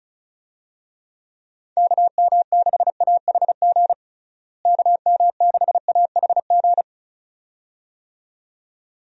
Morse code, heard as a keyed tone, about 36 wpm, 700 Hz. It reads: KM6AHG KM6AHG
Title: USA Call Signs - 6 Characters - Repeated 2x - 35wpm